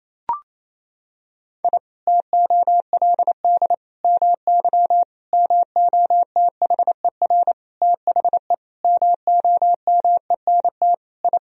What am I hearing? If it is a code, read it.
Morse code, heard as a keyed tone, about 28 wpm, 700 Hz. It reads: I TOLD MY MOTHER THE MOMENT I
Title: Sentences from Top 1000 Words - Review Part 1 - 28wpm